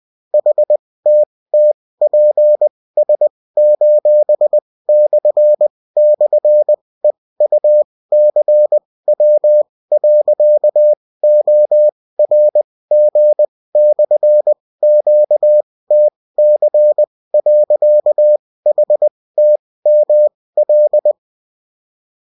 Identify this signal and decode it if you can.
Morse code, heard as a keyed tone, about 20 wpm, spaced as for 16 wpm, 600 Hz. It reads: HTTPS://EUCW.ORG/QTC.HTML